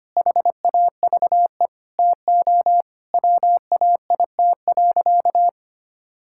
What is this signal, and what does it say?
Morse code, heard as a keyed tone, about 25 wpm, 700 Hz. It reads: HAVE TO WAIT.
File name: practice / 25wpm / Sentences with 3-Word Phrases - 25wpm